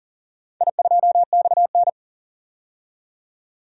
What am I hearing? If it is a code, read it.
Morse code, heard as a keyed tone, about 40 wpm, 700 Hz. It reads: I2XN
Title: DXCC-100 Call Signs - 4 Characters - 40wpm